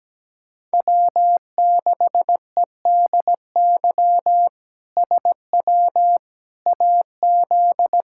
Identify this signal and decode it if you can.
Morse code, heard as a keyed tone, about 17 wpm, 700 Hz. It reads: W6EDY SW AZ
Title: K1USN SST QSO Practice - 17wpm